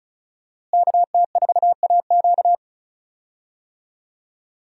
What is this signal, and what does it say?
Morse code, heard as a keyed tone, about 36 wpm, 700 Hz. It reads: KT4AQ
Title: USA Call Signs - 5 Characters - Repeated 3x - 35wpm